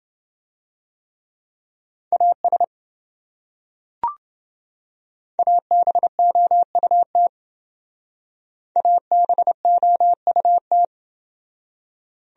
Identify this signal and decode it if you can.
Morse code, heard as a keyed tone, about 27 wpm, 700 Hz. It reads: AS ABOUT ABOUT